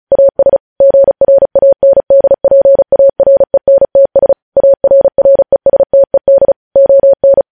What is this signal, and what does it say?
Morse code, heard as a keyed tone, about 36 wpm, 550 Hz. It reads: AS GRANDPARENTS ARRESTED ON
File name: podcast / News headlines-1x-35x35